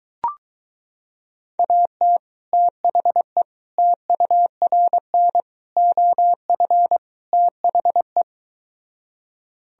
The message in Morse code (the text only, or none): AT THE TURN OF THE